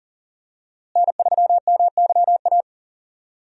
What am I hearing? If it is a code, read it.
Morse code, heard as a keyed tone, about 40 wpm, 700 Hz. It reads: N3MYA
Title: USA Call Signs - 5 Characters - Repeated 2x - 40wpm